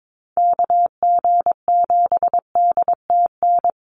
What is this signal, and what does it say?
Morse code, heard as a keyed tone, about 22 wpm, 700 Hz. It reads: KG7DTN
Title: USA Call Signs Repeated 2x 22wpm